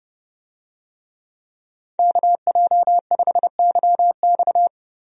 Morse code, heard as a keyed tone, about 32 wpm, 700 Hz. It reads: KJ5YX
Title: USA Call Signs - 5 Characters - Repeated 2x - 30wpm